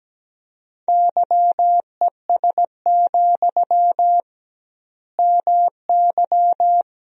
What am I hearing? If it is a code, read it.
Morse code, heard as a keyed tone, about 17 wpm, 700 Hz. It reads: YES, MY
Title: 2x Word-Spacing - Sentences from Top 500 Words - 17wpm